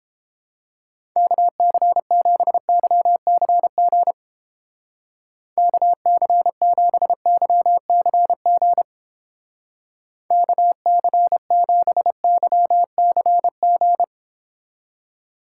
Morse code, heard as a keyed tone, about 31 wpm, 700 Hz. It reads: KC7YCG KC7YCG KC7YCG